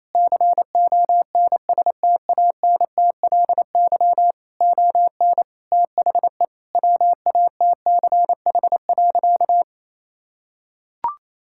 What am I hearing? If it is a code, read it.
Morse code, heard as a keyed tone, about 28 wpm, 700 Hz. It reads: CONSTANTLY ON THE WATCH.
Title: Sentences from Top 1600 Words 28wpm